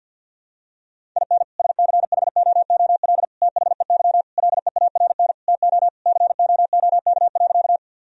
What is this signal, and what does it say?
Morse code, heard as a keyed tone, about 50 wpm, 700 Hz. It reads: IN SCHOOL THEY LEARN TO COOK.
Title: Sentences from Top 1000 Words - Review Part 4 - 50wpm